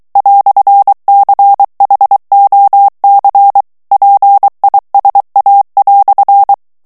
Morse code, heard as a keyed tone, about 23 wpm, 800 Hz. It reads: "CHOC PISA"
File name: FRANCAIS TEXTE CLAIR 1 VITESSE 22 MOTS MINUTES